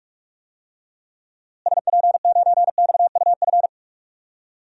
Morse code, heard as a keyed tone, about 46 wpm, 700 Hz. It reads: SP9XUF
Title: DXCC-100 Call Signs - 6 Characters - Repeated 2x - 45wpm